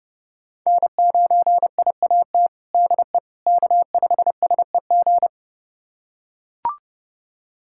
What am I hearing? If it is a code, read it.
Morse code, heard as a keyed tone, about 30 wpm, 700 Hz. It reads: N9IAT DE K5SEG
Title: USA Callsign DE Callsign - 5 Characters - 30wpm